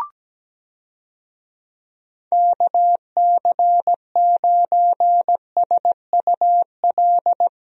Morse code, heard as a keyed tone, about 17 wpm, 700 Hz. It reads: KC9SUL
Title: USA Call Signs - 6 Characters - Repeated 2x - 17wpm